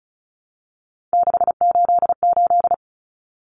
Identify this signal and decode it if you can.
Morse code, heard as a keyed tone, about 34 wpm, 700 Hz. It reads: 688
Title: Numbers - 3-Digit 35wpm